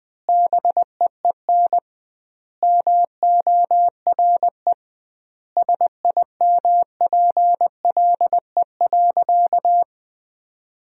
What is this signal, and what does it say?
Morse code, heard as a keyed tone, about 20 wpm, 700 Hz. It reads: BEEN MORE SIMPLE.
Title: 2x Word-Spacing - Sentences from Top 400 Words - 20wpm